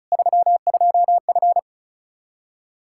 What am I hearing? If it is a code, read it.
Morse code, heard as a keyed tone, about 36 wpm, 700 Hz. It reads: 32F
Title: Practice QSO - Exchange 2 - Info - 35wpm